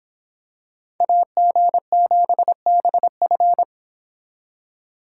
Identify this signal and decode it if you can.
Morse code, heard as a keyed tone, about 26 wpm, 700 Hz. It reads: AG7BF